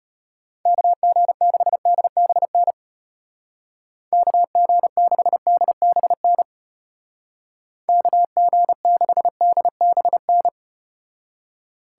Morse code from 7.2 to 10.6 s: KG6DBN

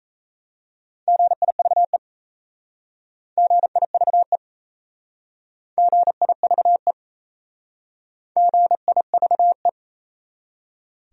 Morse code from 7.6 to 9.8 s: GIVE

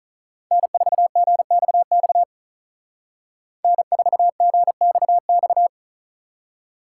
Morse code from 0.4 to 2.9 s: N4GXX

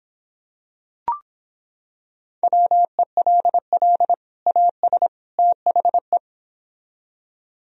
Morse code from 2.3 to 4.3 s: WELL

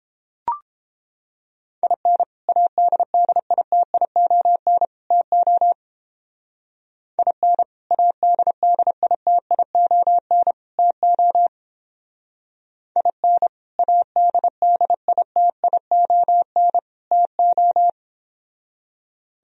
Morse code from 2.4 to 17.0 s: ADDITION TO IN ADDITION TO IN ADDITION

